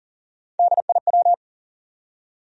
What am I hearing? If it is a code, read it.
Morse code, heard as a keyed tone, about 40 wpm, 700 Hz. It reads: DIW
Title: USA Call Sign Postfixes - 3 Letters 40wpm